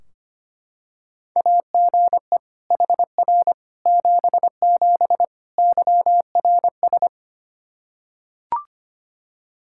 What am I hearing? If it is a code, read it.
Morse code, heard as a keyed tone, about 25 wpm, 700 Hz. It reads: AGE HR 77 YRS
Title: Practice QSO - Exchange 3 - More Info - 25wpm